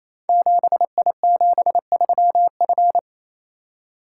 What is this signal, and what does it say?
Morse code, heard as a keyed tone, about 29 wpm, 700 Hz. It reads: 7I73F
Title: Code Groups - 5 Letter:Numbers - 28wpm